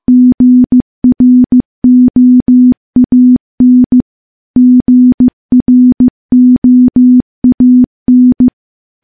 Morse code, heard as a keyed tone, about 15 wpm, 250 Hz. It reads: GROAN GROAN
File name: morse / morse-training-15WPM